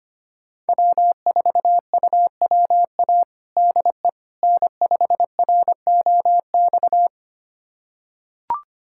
Morse code, heard as a keyed tone, about 25 wpm, 700 Hz. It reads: W4UWA DE N5ROX
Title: USA Callsign DE Callsign - 5 Characters - 25wpm